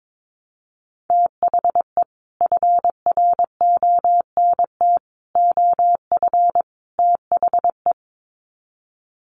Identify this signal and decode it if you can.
Morse code, heard as a keyed tone, about 22 wpm, 700 Hz. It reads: THE FRONT OF THE
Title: Sets of 4 Words 22wpm